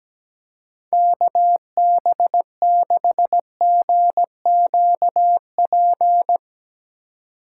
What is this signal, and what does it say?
Morse code, heard as a keyed tone, about 17 wpm, 700 Hz. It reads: KB6GQP
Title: USA Call Signs - 6 Characters - 4x Character Spacing - 17wpm